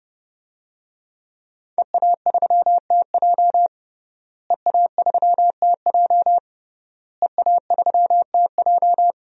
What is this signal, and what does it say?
Morse code, heard as a keyed tone, about 32 wpm, 700 Hz. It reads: EA3TJ EA3TJ EA3TJ
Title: DXCC-100 Call Signs - 5 Characters - Repeated 3x - 30wpm